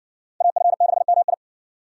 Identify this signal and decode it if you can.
Morse code, heard as a keyed tone, about 60 wpm, 700 Hz. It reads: AF6MI